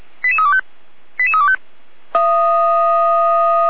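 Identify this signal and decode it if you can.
Five-tone selective call (ZVEI), heard as quick runs of five tones, tones of about 70 ms. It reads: 88325 88325
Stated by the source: ZVEI five-tone call 88283 received by SDR.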